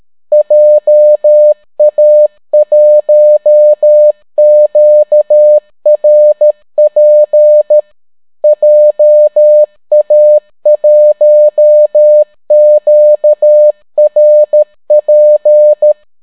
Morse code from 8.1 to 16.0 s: JA1QRP